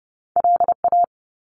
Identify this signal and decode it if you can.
Morse code, heard as a keyed tone, about 30 wpm, 700 Hz. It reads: LA
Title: US States 30wpm